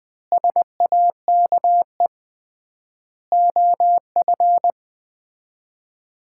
Morse code, heard as a keyed tone, about 21 wpm, 700 Hz. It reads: SAKE OF
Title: 3x Word-Spacing - Sets of 4 Words - 20wpm